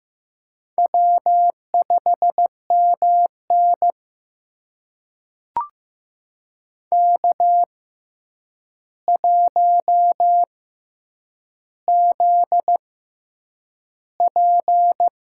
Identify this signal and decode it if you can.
Morse code, heard as a keyed tone, about 15 wpm, spaced as for 8 wpm, 700 Hz. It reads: W5MN K1ZP
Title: USA Call Signs - 4 Characters - 6x Character Spacing - 15wpm